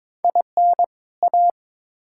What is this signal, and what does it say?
Morse code, heard as a keyed tone, about 22 wpm, 700 Hz. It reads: IN A
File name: Build A Phrase - 2-Word Phrases - 22wpm